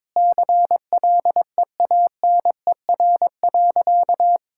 Morse code, heard as a keyed tone, about 22 wpm, 700 Hz. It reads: CLEANER.
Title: Pangram Challenge v2 - 22wpm